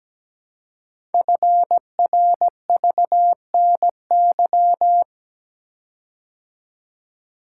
Morse code, heard as a keyed tone, about 17 wpm, 700 Hz. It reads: FRVNY
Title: Code Groups - 5 Letters - 17wpm